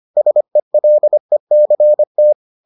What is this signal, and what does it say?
Morse code, heard as a keyed tone, about 25 wpm, 600 Hz. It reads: SELECT